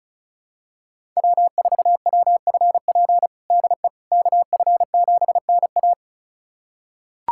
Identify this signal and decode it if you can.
Morse code, heard as a keyed tone, about 36 wpm, 700 Hz. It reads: W4WFP DE KF7NA
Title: USA Callsign DE Callsign - 5 Characters - 35wpm